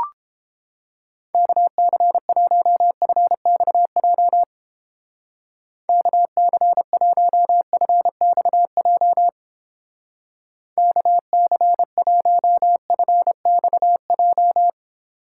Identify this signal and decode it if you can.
Morse code, heard as a keyed tone, about 31 wpm, 700 Hz. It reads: KC1FXJ KC1FXJ KC1FXJ